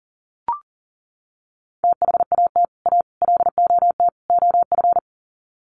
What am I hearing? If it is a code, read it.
Morse code, heard as a keyed tone, about 40 wpm, 700 Hz. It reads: THAT A LOT OF
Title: Sets of 4 Words 40wpm